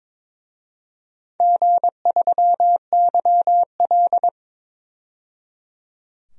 Morse code, heard as a keyed tone, about 22 wpm, 700 Hz. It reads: G3YL
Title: DXCC-100 Call Signs - 4 Characters - 22wpm